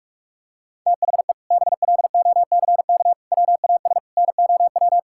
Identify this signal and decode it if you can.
Morse code, heard as a keyed tone, about 46 wpm, 700 Hz. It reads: THE BLOCK WAS NOW
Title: Sentences from Top 1600 Words 45wpm